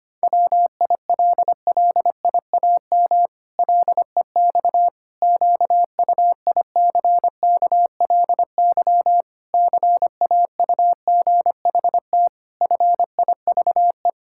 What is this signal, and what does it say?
Morse code, heard as a keyed tone, about 25 wpm, 700 Hz. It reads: WILLIAM LEX QUICKLY CAUGHT FIVE
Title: Pangram Challenge v2 - 25wpm